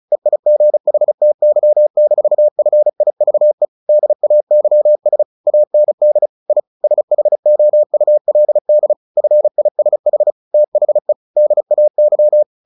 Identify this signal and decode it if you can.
Morse code, heard as a keyed tone, about 34 wpm, 600 Hz. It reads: EIGHTY-FIVE DAYS AND I SHOULD FISH THE DAY
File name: the old man and the sea / the old man and the sea 1 35wpm